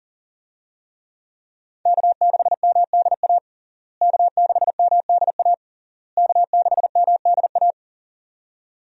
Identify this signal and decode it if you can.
Morse code, heard as a keyed tone, about 40 wpm, 700 Hz. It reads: K6MDA K6MDA K6MDA
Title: USA Call Signs - 5 Characters - Repeated 3x - 40wpm